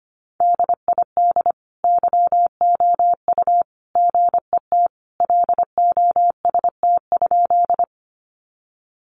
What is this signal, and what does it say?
Morse code, heard as a keyed tone, about 25 wpm, 700 Hz. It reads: DID YOU GET LOST?
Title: Sentences from Top 1400 Words 25wpm